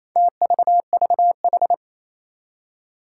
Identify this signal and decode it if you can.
Morse code, heard as a keyed tone, about 27 wpm, 700 Hz. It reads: TVVH
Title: Four Challenging Letter-Number Combinations 28wpm